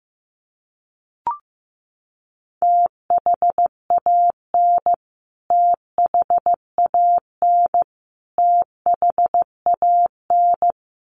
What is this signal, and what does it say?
Morse code, heard as a keyed tone, about 15 wpm, 700 Hz. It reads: THAN THAN THAN
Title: Top 200 Words Repeated 15wpm